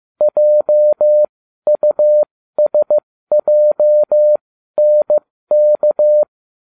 Morse code, heard as a keyed tone, about 15 wpm, spaced as for 12 wpm, 600 Hz. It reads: JUSJNK